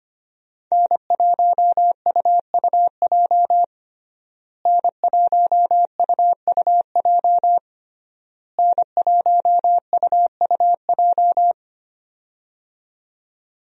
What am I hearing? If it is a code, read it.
Morse code, heard as a keyed tone, about 25 wpm, 700 Hz. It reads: N1UUJ N1UUJ N1UUJ
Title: USA Call Signs - 5 Characters - Repeated 3x - 25wpm